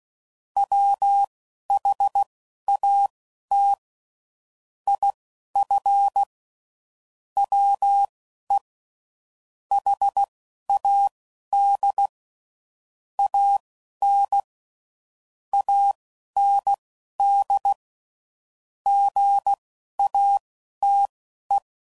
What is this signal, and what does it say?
Morse code, heard as a keyed tone, about 16 wpm, spaced as for 11 wpm, 800 Hz. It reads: WHAT IF WE HAD AN AND GATE
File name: CW Code Groups @10WPM / Technical Subjects / AND Gate